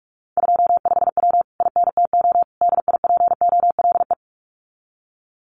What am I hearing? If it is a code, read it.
Morse code, heard as a keyed tone, about 46 wpm, 700 Hz. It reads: 25W INTO DIPOLE
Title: Common QSO Phrases Encore 45wpm